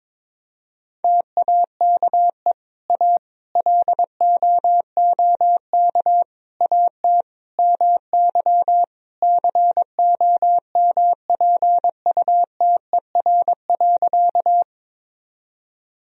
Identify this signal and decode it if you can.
Morse code, heard as a keyed tone, about 22 wpm, 700 Hz. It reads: TAKE A LOOK AT MY COMPUTER.
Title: Sentences from Top 1000 Words - Review Part 3 - 22wpm